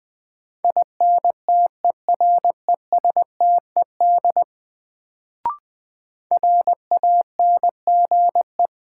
Morse code, heard as a keyed tone, about 20 wpm, 700 Hz. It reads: INTERESTED RANGE
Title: Sentences from Top 900 Words - New Word Drill - 20wpm